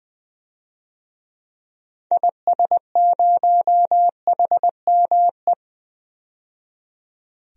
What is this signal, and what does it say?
Morse code, heard as a keyed tone, about 20 wpm, 700 Hz. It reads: IS0HME
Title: DXCC-100 Call Signs - 6 Characters - 20wpm